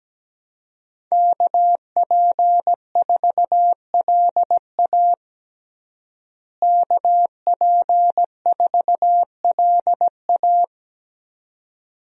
Morse code, heard as a keyed tone, about 17 wpm, 700 Hz. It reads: KP4LA KP4LA
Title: USA Call Signs - 5 Characters - Repeated 2x - 17wpm